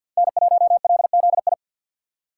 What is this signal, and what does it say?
Morse code, heard as a keyed tone, about 48 wpm, 700 Hz. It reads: N1LZI
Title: USA Call Signs - 5 Characters - 4x Character Spacing - 50wpm